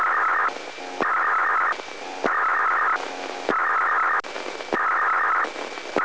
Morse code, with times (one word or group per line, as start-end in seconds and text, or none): none